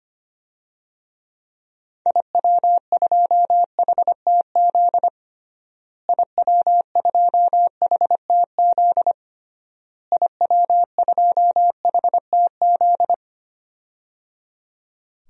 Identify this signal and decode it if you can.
Morse code, heard as a keyed tone, about 25 wpm, 700 Hz. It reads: IW2HTZ IW2HTZ IW2HTZ
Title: DXCC-100 Call Signs - 6 Characters - Repeated 3x - 25wpm